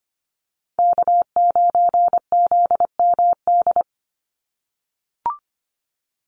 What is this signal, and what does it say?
Morse code, heard as a keyed tone, about 25 wpm, 700 Hz. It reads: K9ZMD
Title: NA SOTA Chaser Callsigns 25wpm